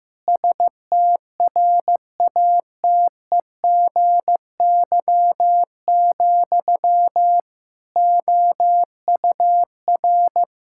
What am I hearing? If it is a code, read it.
Morse code, heard as a keyed tone, about 15 wpm, 700 Hz. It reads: STRATEGY, OUR
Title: Sentences from Top 1000 Words - Review Part 5 - 15wpm